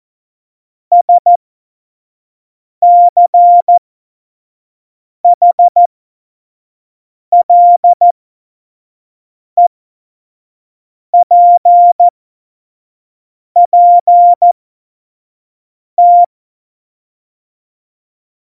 Morse code, heard as a keyed tone, about 14 wpm, spaced as for 5 wpm, 700 Hz. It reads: SCHLEPPT
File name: Morsekurs / Witze / Witz 17.Juli 2018, 25 F70